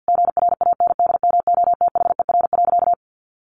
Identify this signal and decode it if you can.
Morse code, heard as a keyed tone, about 50 wpm, 700 Hz. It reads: GRANDMOTHER.
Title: Sentences from Top 1200 Words 50wpm